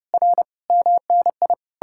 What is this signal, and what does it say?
Morse code, heard as a keyed tone, about 37 wpm, 700 Hz. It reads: R MNI
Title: Speed-Racing Common QSO Phrases - 30wpm